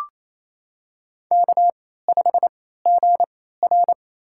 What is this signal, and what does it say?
Morse code, heard as a keyed tone, about 28 wpm, spaced as for 16 wpm, 700 Hz. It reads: K5GR
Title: USA Call Signs - 3x Character Spacing - 28wpm